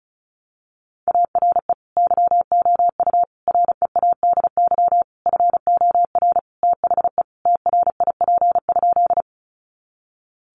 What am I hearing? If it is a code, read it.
Morse code, heard as a keyed tone, about 34 wpm, 700 Hz. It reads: ARE YOU READY FOR THE TRIP?
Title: Sentences from Top 1300 Words 35wpm